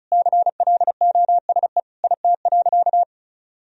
Morse code, heard as a keyed tone, about 34 wpm, 700 Hz. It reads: CLOSE IT.
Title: Sentences from Top 1000 Words - Review Part 1 - 35wpm